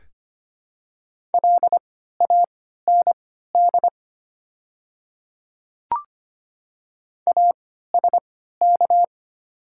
Morse code, heard as a keyed tone, about 25 wpm, spaced as for 14 wpm, 700 Hz. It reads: LAND ASK
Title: 3x Character Spacing - Top 200 Words - 25wpm